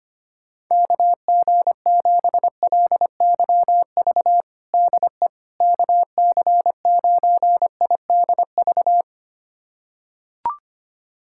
Morse code, heard as a keyed tone, about 25 wpm, 700 Hz. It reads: KG7LYV DE KC9IDV
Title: USA Callsign DE Callsign - 6 Characters - 25wpm